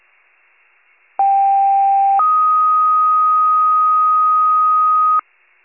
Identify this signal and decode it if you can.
Two-tone page, heard as a one-second tone, then a three-second tone, about 790 then 1250 Hz. It is heard from 1.2 to 5.2 s.